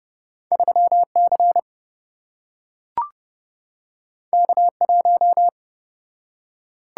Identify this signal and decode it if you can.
Morse code, heard as a keyed tone, about 27 wpm, 700 Hz. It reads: 3C K1